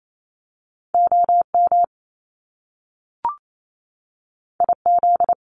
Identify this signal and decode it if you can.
Morse code, heard as a keyed tone, about 28 wpm, 700 Hz. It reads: O M I Z
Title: Top Two Letter Combinations in Words 28wpm